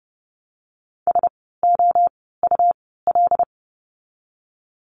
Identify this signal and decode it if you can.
Morse code, heard as a keyed tone, about 30 wpm, spaced as for 17 wpm, 700 Hz. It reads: SOUL
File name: Lesson 17f - Letter u - Words 30wpm